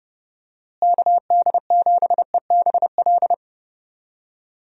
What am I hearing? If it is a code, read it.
Morse code, heard as a keyed tone, about 29 wpm, 700 Hz. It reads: KD7EBL
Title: USA Call Signs - 6 Characters - Repeated 3x - 30wpm